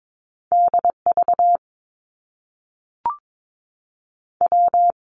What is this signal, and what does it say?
Morse code, heard as a keyed tone, about 22 wpm, spaced as for 3 wpm, 700 Hz. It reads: D VW
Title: Top Two Letter Combinations in Words 22wpm